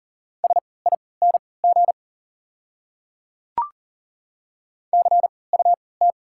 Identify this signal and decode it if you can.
Morse code, heard as a keyed tone, about 40 wpm, spaced as for 23 wpm, 690 Hz. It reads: SING CUT
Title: 3x Character Spacing - Top 500 Words - 40wpm